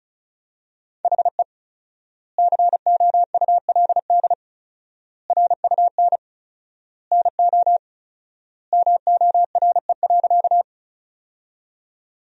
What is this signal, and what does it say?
Morse code, heard as a keyed tone, about 34 wpm, 700 Hz. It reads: HE COULD RUN NO MORE.
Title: 4x Word-Spacing - Sentences from Top 300 Words - 35wpm